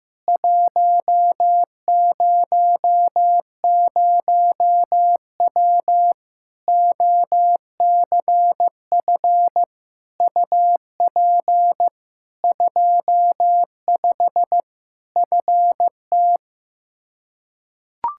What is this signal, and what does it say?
Morse code, heard as a keyed tone, about 15 wpm, 700 Hz. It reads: 100W OCF UP 25 FT
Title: Practice QSO - Exchange 2 - Info - 15wpm